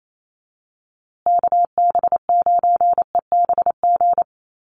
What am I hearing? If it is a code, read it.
Morse code, heard as a keyed tone, about 28 wpm, 700 Hz. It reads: KB9EBG
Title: USA Call Signs Repeated 2x 28wpm